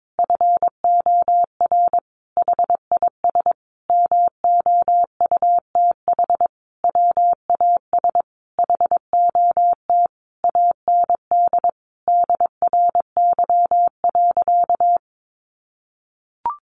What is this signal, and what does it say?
Morse code, heard as a keyed tone, about 22 wpm, 700 Hz. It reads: FOR HIS MOUTH WAS HOT AND DRY.
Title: Sentences from Top 1500 Words 22wpm